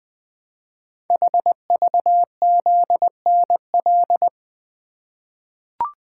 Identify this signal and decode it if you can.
Morse code, heard as a keyed tone, about 20 wpm, 700 Hz. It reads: HVZNL
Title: Code Groups - 5 Letters - 20wpm